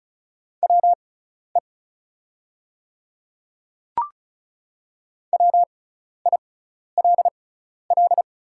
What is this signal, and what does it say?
Morse code, heard as a keyed tone, about 35 wpm, spaced as for 12 wpm, 690 Hz. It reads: WE WILL
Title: 6x Character Spacing - Top 100 Words - 35wpm